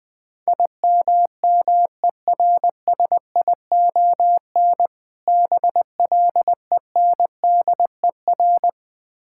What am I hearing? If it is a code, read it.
Morse code, heard as a keyed tone, about 20 wpm, 700 Hz. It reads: IMMERSION BLENDER
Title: Difficult Words - Speed-Racing with Hints - 20wpm